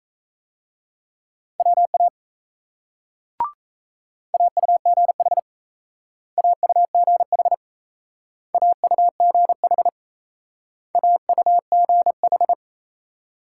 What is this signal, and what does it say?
Morse code, heard as a keyed tone, about 37 wpm, 700 Hz. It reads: WA AUGH AUGH AUGH AUGH